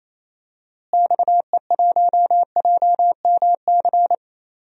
Morse code, heard as a keyed tone, about 27 wpm, 700 Hz. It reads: XE1JMC
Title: DXCC-100 Call Signs - 6 Characters - Repeated 2x - 28wpm